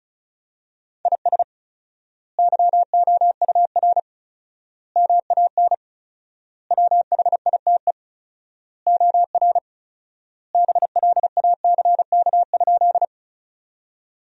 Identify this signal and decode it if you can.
Morse code, heard as a keyed tone, about 36 wpm, 700 Hz. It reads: IS YOUR MAN WHITE OR BLACK?
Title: 4x Word-Spacing - Sentences from Top 400 Words - 35wpm